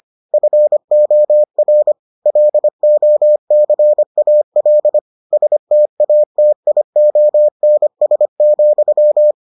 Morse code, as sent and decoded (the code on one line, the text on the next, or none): ..-. --- .-. / .-.. --- -.-. .- .-.. / ... - .- - .. --- -. ... --..--
FOR LOCAL STATIONS,